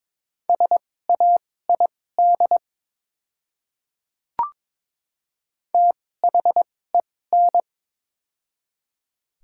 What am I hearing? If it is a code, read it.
Morse code, heard as a keyed tone, about 23 wpm, spaced as for 16 wpm, 700 Hz. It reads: SAID THEN
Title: 2x Character Spacing - Top 100 Words - 22wpm